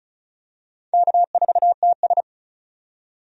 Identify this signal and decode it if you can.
Morse code, heard as a keyed tone, about 34 wpm, 700 Hz. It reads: K4TS
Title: USA Call Signs - 4 Characters - 35wpm